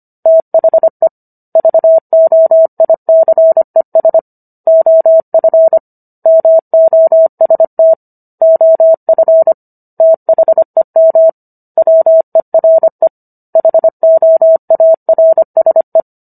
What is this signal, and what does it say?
Morse code, heard as a keyed tone, about 25 wpm, 650 Hz. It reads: THE VOICES OF MOST OF THEM WERE HOARSE
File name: Book One / 16-Chapter 16 - 25 WPM